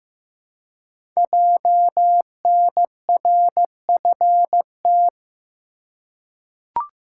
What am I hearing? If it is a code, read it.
Morse code, heard as a keyed tone, about 15 wpm, 700 Hz. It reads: JNRFT
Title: Code Groups - 5 Letter:Numbers - 15wpm